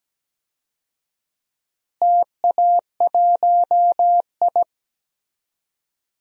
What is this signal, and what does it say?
Morse code, heard as a keyed tone, about 17 wpm, 700 Hz. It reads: TA1I
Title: DXCC-100 Call Signs - 4 Characters - Repeated 3x - 17wpm